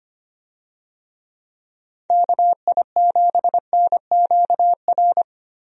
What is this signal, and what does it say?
Morse code, heard as a keyed tone, about 25 wpm, 700 Hz. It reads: KI7NQR
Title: USA Call Signs - 6 Characters - 25wpm